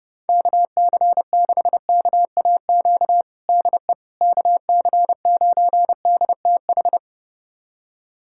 Morse code, heard as a keyed tone, about 30 wpm, 700 Hz. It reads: KC6KAQ DE KC9DTH
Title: USA Callsign DE Callsign - 6 Characters - 30wpm